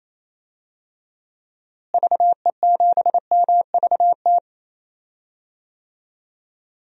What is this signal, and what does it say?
Morse code, heard as a keyed tone, about 28 wpm, 700 Hz. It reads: VE7MVT
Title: DXCC-100 Call Signs - 6 Characters - Repeated 2x - 28wpm